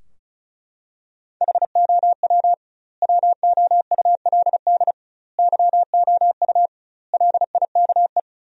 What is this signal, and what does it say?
Morse code, heard as a keyed tone, about 36 wpm, 700 Hz. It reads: HOW WOULD YOU LIKE
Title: 2x Word-Spacing - Sentences from Top 100 Words - 35wpm